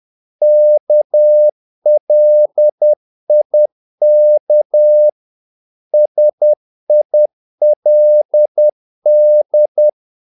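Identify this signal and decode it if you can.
Morse code, heard as a keyed tone, about 10 wpm, 600 Hz. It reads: KLIK SILD
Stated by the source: Morse practice of short Danish words at 10 wpm, from CW med Gnister.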